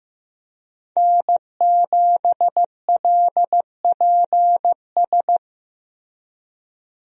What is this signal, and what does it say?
Morse code, heard as a keyed tone, about 15 wpm, 700 Hz. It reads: N7LPS
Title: USA Call Signs - 5 Characters - Repeated 3x - 15wpm